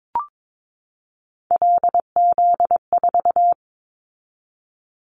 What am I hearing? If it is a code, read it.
Morse code, heard as a keyed tone, about 22 wpm, 700 Hz. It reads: LZ4
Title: Callsign Prefixes for Top 50 DXCC - 22wpm